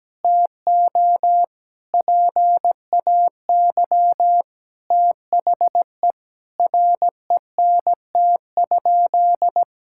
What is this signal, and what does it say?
Morse code, heard as a keyed tone, about 17 wpm, 700 Hz. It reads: TO PAY THE RENT?
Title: Sentences from Top 1000 Words - Review Part 5 - 17wpm